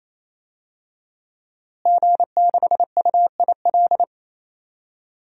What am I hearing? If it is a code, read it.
Morse code, heard as a keyed tone, about 27 wpm, 700 Hz. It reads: G6UIL
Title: DXCC-100 Call Signs - 5 Characters - Repeated 3x - 28wpm